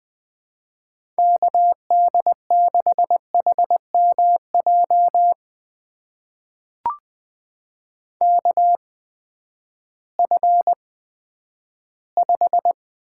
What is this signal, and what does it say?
Morse code, heard as a keyed tone, about 20 wpm, spaced as for 11 wpm, 700 Hz. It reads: KD6HMJ KF5
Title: USA Call Signs - 6 Characters - 8x Character Spacing - 20wpm